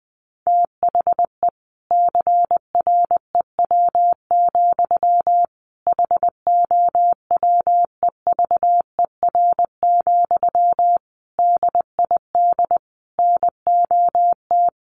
Morse code, heard as a keyed tone, about 20 wpm, 700 Hz. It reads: THE CREW, HOWEVER, DID NOT
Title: Sentences from Top 1500 Words 20wpm